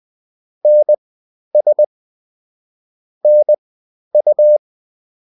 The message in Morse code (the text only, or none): NS NU